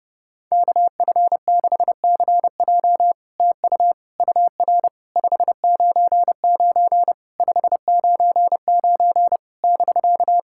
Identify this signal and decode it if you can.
Morse code, heard as a keyed tone, about 30 wpm, 700 Hz. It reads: KF6CJ TU UR 599 599 <BK>
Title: SOTA QSO Practice 30wpm